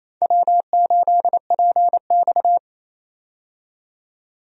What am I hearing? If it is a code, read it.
Morse code, heard as a keyed tone, about 28 wpm, 700 Hz. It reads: W8PX
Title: USA Call Signs - 4 Characters - 28wpm